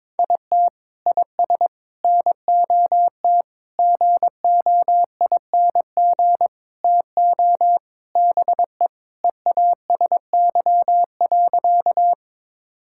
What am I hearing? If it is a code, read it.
Morse code, heard as a keyed tone, about 22 wpm, 700 Hz. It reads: IT IS NOT GOING TO BE EASY.
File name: practice / 22wpm / Sentences with 4-Word Phrases - 22wpm